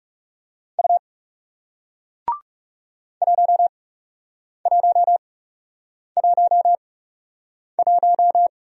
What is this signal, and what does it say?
Morse code, heard as a keyed tone, about 37 wpm, spaced as for 8 wpm, 700 Hz. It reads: U 1111
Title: Speed-Racing Single Letter-Number 30wpm